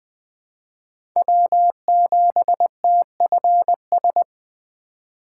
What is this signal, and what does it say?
Morse code, heard as a keyed tone, about 20 wpm, 700 Hz. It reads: W7TFS
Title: USA Call Signs - 5 Characters - 4x Character Spacing - 20wpm